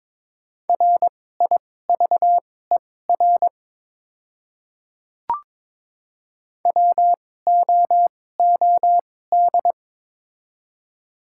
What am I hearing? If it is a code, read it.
Morse code, heard as a keyed tone, about 22 wpm, spaced as for 16 wpm, 700 Hz. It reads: RIVER WOOD
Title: 2x Character Spacing - Top 500 Words - 22wpm